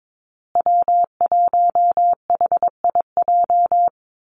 Morse code, heard as a keyed tone, about 22 wpm, 700 Hz. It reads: W1HIJ
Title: Call Signs Repeated 3x - 22wpm